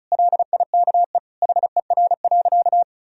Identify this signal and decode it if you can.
Morse code, heard as a keyed tone, about 34 wpm, 700 Hz. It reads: LIKE HER.
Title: Sentences from Top 1000 Words - Review Part 2 - 35wpm